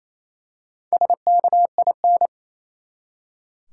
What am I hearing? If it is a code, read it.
Morse code, heard as a keyed tone, about 28 wpm, 700 Hz. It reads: SKIN
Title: Four Letter Words - 28wpm